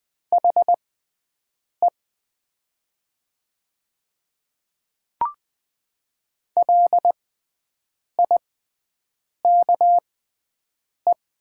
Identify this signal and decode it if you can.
Morse code, heard as a keyed tone, about 20 wpm, spaced as for 7 wpm, 700 Hz. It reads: HE LIKE
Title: 6x Character Spacing - Top 100 Words - 20wpm